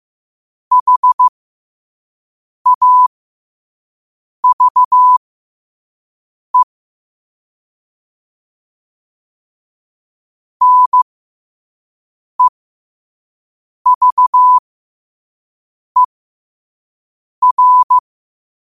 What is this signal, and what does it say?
Morse code, heard as a keyed tone, about 15 wpm, spaced as for 5 wpm, 1000 Hz. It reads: HAVE NEVER
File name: qotdfiles / 5wpm